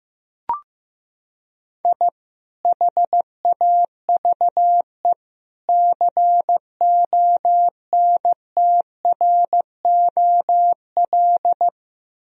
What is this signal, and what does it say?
Morse code, heard as a keyed tone, about 16 wpm, 700 Hz. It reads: I HAVE CONTROL
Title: Sentences from Top 1000 Words - Review Part 3 - 15wpm